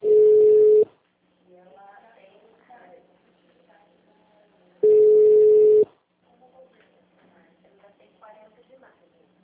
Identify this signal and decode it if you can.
Ringback tone (Europe, 425 Hz), with speech.